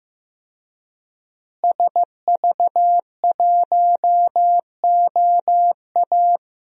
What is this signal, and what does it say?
Morse code, heard as a keyed tone, about 15 wpm, 700 Hz. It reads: SV1OA
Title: DXCC-100 Call Signs - 5 Characters - Repeated 3x - 15wpm